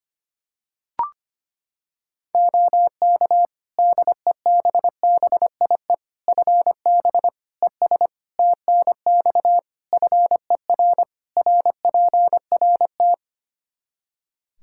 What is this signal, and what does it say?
Morse code, heard as a keyed tone, about 25 wpm, 700 Hz. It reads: OK DEBBIE FB ES TNX FER RPRT
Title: Practice QSO - Exchange 2 - Info - 25wpm